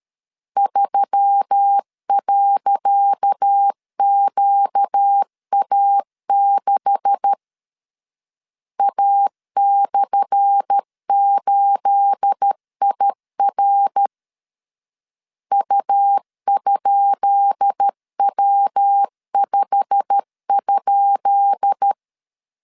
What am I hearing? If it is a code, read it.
Morse code, heard as a keyed tone, about 13 wpm, 800 Hz. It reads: 3.QA6 A/8IR U?W5?
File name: LEZIONI / 10 SIGNS / 10C DOT 12wpm 2451